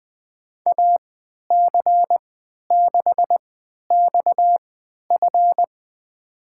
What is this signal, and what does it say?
Morse code, heard as a keyed tone, about 20 wpm, spaced as for 11 wpm, 700 Hz. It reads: AC6XF
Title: USA Call Signs - 3x Character Spacing - 20wpm